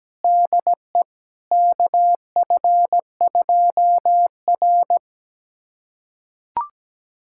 Morse code, heard as a keyed tone, about 17 wpm, 700 Hz. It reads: DE KF2R
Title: USA Callsign DE Callsign - 4 Characters - 17wpm